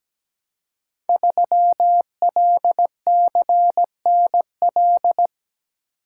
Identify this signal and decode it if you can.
Morse code, heard as a keyed tone, about 17 wpm, 700 Hz. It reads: 3LCNL
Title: Code Groups - 5 Letter:Numbers - 17wpm